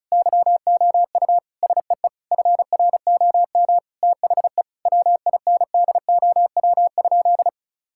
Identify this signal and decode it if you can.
Morse code, heard as a keyed tone, about 35 wpm, 700 Hz. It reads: YOU SEE FROM THE WINDOW?
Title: Sentences from Top 1600 Words 35wpm